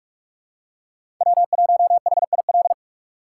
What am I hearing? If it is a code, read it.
Morse code, heard as a keyed tone, about 47 wpm, 700 Hz. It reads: W1HIL
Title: USA Call Signs - 3x Character Spacing - 45wpm